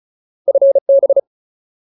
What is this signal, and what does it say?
Morse code, heard as a keyed tone, about 34 wpm, 550 Hz. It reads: FB
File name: Ultimate Word List / 35WPM Ultimate